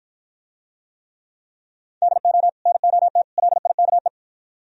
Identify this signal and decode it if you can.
Morse code, heard as a keyed tone, about 53 wpm, 700 Hz. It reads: DO NOT LIKE